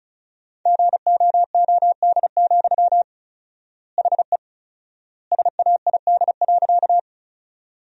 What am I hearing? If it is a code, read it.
Morse code, heard as a keyed tone, about 36 wpm, 700 Hz. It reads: GOOD, HE SAID.
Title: 4x Word-Spacing - Sentences from Top 200 Words - 35wpm